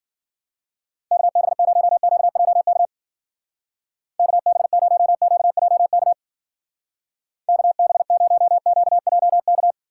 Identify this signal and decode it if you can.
Morse code, heard as a keyed tone, about 50 wpm, 700 Hz. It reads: KB0QJK KB0QJK KB0QJK
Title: Speed Racing - USA 2x3 Call Signs 40wpm